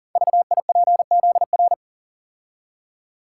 Morse code, heard as a keyed tone, about 40 wpm, 700 Hz. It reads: VIPZR
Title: Code Groups - 5 Letters - 40wpm